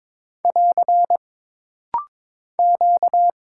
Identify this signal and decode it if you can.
Morse code, heard as a keyed tone, about 22 wpm, 700 Hz. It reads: <AR> Q
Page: This rapid-fire timing is geared to push your ICR (Instant Character Recognition) skills.